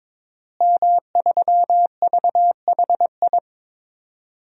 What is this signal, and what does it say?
Morse code, heard as a keyed tone, about 23 wpm, 700 Hz. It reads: M3VHI
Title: DXCC-100 Call Signs - 5 Characters - Repeated 2x - 22wpm